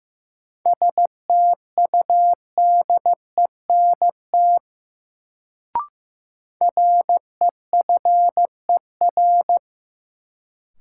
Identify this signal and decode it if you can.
Morse code, heard as a keyed tone, about 15 wpm, 700 Hz. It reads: STUDENT REFER
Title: Sentences from Top 1000 Words - New Word Drill - 15wpm